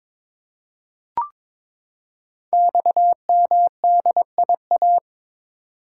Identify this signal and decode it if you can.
Morse code, heard as a keyed tone, about 22 wpm, 700 Hz. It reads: XMDIA
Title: Code Groups - 5 Letters - 22wpm